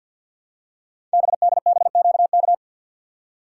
Morse code, heard as a keyed tone, about 52 wpm, 700 Hz. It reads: BDBQK
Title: Code Groups - 5 Letters - 50wpm